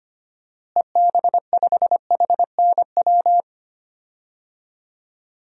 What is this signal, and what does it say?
Morse code, heard as a keyed tone, about 25 wpm, 700 Hz. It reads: EB5HNW
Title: DXCC-100 Call Signs - 6 Characters - Repeated 2x - 25wpm